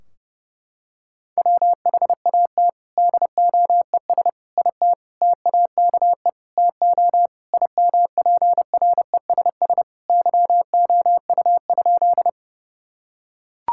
Morse code, heard as a keyed tone, about 30 wpm, 700 Hz. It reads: WHAT DOES IT TAKE TO IMPRESS YOU?
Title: Sentences from Top 1600 Words 30wpm